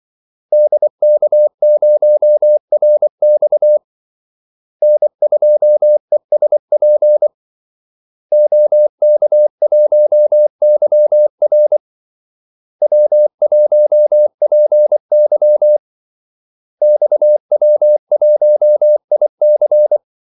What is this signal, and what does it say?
Morse code, heard as a keyed tone, about 24 wpm, 600 Hz. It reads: DK0RX N2ESP OK1YR W1PY XW1IC
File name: Callsigns-24wpm